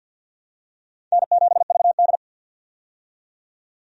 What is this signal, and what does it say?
Morse code, heard as a keyed tone, about 51 wpm, 700 Hz. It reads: N7VD
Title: USA Call Signs - 4 Characters - 50wpm